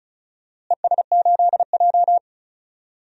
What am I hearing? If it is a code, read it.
Morse code, heard as a keyed tone, about 34 wpm, 700 Hz. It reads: ES8J
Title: DXCC-100 Call Signs - 4 Characters - Repeated 2x - 35wpm